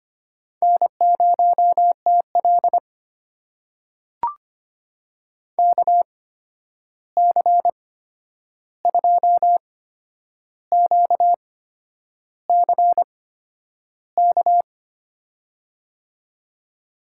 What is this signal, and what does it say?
Morse code, heard as a keyed tone, about 25 wpm, spaced as for 9 wpm, 700 Hz. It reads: N0TL KC2QCK
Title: USA Call Signs - 8x Character Spacing - 25wpm